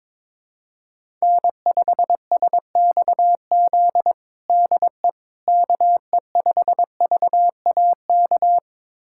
Morse code, heard as a keyed tone, about 22 wpm, 700 Hz. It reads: N5SXZ DE KE5VAK
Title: USA Callsign DE Callsign 22wpm